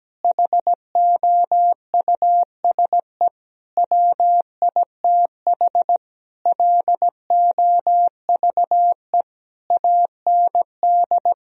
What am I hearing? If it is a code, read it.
Morse code, heard as a keyed tone, about 17 wpm, 700 Hz. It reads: HOUSE WITH LOVE AND
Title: Sentences from Top 1000 Words - Review Part 3 - 17wpm